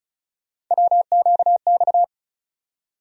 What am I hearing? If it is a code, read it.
Morse code, heard as a keyed tone, about 34 wpm, 700 Hz. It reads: WQX
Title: USA Call Sign Postfixes - 3 Letters 35wpm